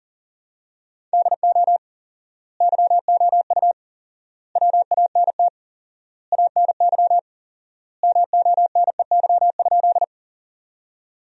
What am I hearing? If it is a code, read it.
Morse code, heard as a keyed tone, about 40 wpm, 700 Hz. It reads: DO YOU WANT ANY MONEY?
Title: 4x Word-Spacing - Sentences from Top 400 Words - 40wpm